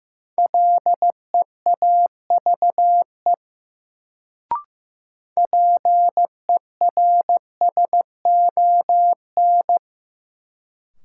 Morse code, heard as a keyed tone, about 15 wpm, 700 Hz. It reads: LEAVE PERSON
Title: Sentences from Top 400 Words - New Word Drill - 15wpm